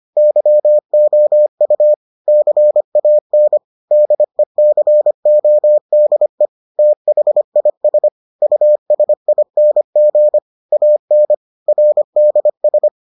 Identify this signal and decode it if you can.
Morse code, heard as a keyed tone, about 25 wpm, 600 Hz. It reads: YOU CAN DECODE THIS USING AN RDS